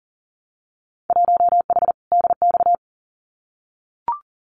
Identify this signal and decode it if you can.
Morse code, heard as a keyed tone, about 42 wpm, 700 Hz. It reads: 1H DX
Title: Winter Field Day Exchanges 40wpm